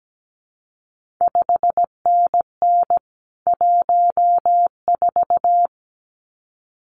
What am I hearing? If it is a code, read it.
Morse code, heard as a keyed tone, about 17 wpm, 700 Hz. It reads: 5NN 14
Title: CQ WW DX Contest Exchange - 17wpm